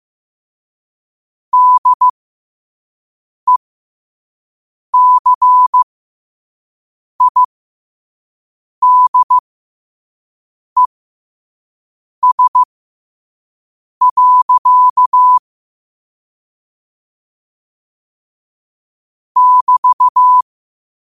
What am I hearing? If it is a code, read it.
Morse code, heard as a keyed tone, about 15 wpm, spaced as for 5 wpm, 1000 Hz. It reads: DECIDES. <BT>